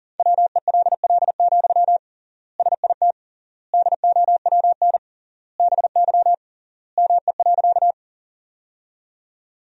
Morse code, heard as a keyed tone, about 40 wpm, 700 Hz. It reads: WELL, SIT DOWN BY ME.
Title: 3x Word-Spacing - Sentences from Top 500 Words - 40wpm